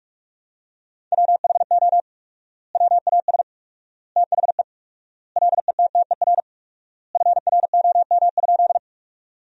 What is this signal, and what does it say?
Morse code, heard as a keyed tone, about 46 wpm, 700 Hz. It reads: WHO WAS THE LETTER FROM?
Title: 4x Word-Spacing - Sentences from Top 300 Words - 45wpm